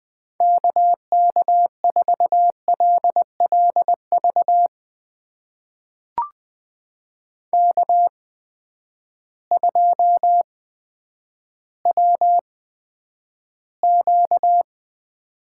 Morse code, heard as a keyed tone, about 20 wpm, spaced as for 10 wpm, 700 Hz. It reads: KK4LLV K2WQ
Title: USA Call Signs - 8x Character Spacing - 20wpm